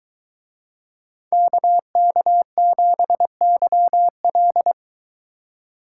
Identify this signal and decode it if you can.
Morse code, heard as a keyed tone, about 23 wpm, 700 Hz. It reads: KK7YL